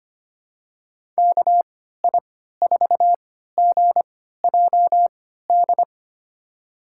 Morse code, heard as a keyed tone, about 25 wpm, spaced as for 14 wpm, 690 Hz. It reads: KI4GJD
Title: USA Call Signs - 6 Characters - 3x Character Spacing - 25wpm